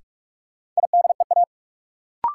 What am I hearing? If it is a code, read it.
Morse code, heard as a keyed tone, about 46 wpm, 700 Hz. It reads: IDEA
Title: Sets of 3 Words - New Word Drill - 45wpm